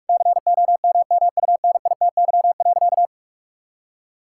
Morse code, heard as a keyed tone, about 46 wpm, 700 Hz. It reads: COMMUNITY.
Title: Sentences from Top 1000 Words - Review Part 5 - 45wpm